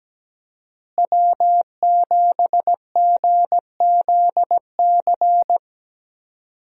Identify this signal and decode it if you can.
Morse code, heard as a keyed tone, about 17 wpm, 700 Hz. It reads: W7GZC
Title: USA Call Signs - 5 Characters - Repeated 2x - 17wpm